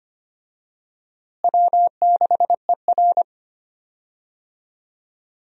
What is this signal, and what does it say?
Morse code, heard as a keyed tone, about 25 wpm, 700 Hz. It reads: W6ER
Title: USA Call Signs - 4 Characters - Repeated 2x - 25wpm